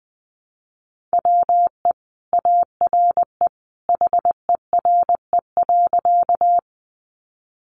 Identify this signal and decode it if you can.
Morse code, heard as a keyed tone, about 20 wpm, 700 Hz. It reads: WE ARE HERE.
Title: Sentences from Top 200 Words 20wpm